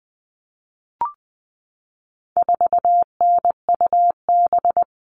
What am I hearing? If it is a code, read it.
Morse code, heard as a keyed tone, about 20 wpm, 700 Hz. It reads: 4NUB
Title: All Cut Numbers - 4-Digit 20wpm